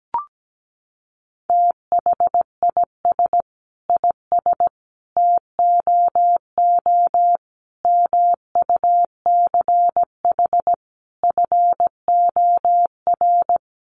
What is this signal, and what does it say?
Morse code, heard as a keyed tone, about 17 wpm, 700 Hz. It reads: THIS IS TOO MUCH FOR
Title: A Sound Of Thunder 17wpm